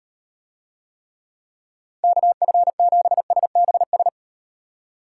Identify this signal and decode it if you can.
Morse code, heard as a keyed tone, about 36 wpm, 700 Hz. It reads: KF7SBS